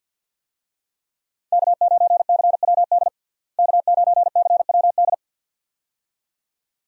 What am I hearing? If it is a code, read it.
Morse code, heard as a keyed tone, about 47 wpm, 700 Hz. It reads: K9CWD K9CWD
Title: USA Call Signs - 5 Characters - Repeated 2x - 50wpm